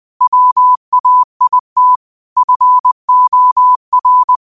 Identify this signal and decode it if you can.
Morse code, heard as a keyed tone, about 20 wpm, 1000 Hz. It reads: WAIT FOR
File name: qotdfiles / 20wpm